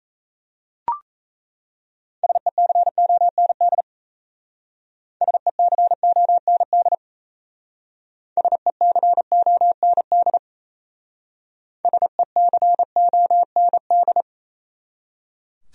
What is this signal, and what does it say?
Morse code, heard as a keyed tone, about 36 wpm, 700 Hz. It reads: SECOND SECOND SECOND SECOND